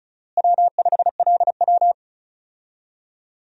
Morse code, heard as a keyed tone, about 36 wpm, 700 Hz. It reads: W5LW
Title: USA Call Signs - 4 Characters - Repeated 3x - 35wpm